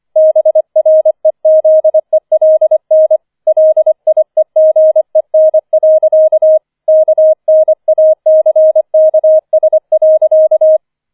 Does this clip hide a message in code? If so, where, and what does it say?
Yes: Morse code from 0.2 to 10.8 s, heard as a keyed tone, about 25 wpm, 630 Hz. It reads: BREZELN LIEGEN. KNACKS.